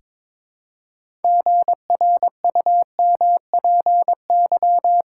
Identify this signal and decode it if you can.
Morse code, heard as a keyed tone, about 22 wpm, 700 Hz. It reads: GRUMPY
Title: Pangram Challenge v2 - 22wpm